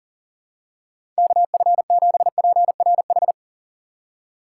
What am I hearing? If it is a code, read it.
Morse code, heard as a keyed tone, about 38 wpm, 700 Hz. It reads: KF7PRH
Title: USA Call Signs - 2x Character Spacing - 40wpm